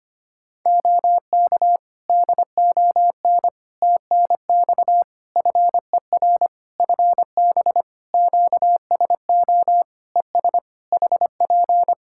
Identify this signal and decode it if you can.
Morse code, heard as a keyed tone, about 25 wpm, 700 Hz. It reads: OK DON TNX FER FB QSO ES HP
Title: Practice QSO - Exchange 4 - Ending - 25wpm